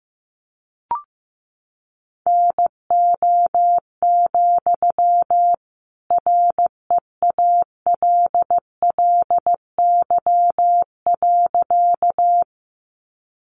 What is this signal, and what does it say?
Morse code, heard as a keyed tone, about 15 wpm, 700 Hz. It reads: NO, REALLY.
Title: Sentences from Top 600 Words 15wpm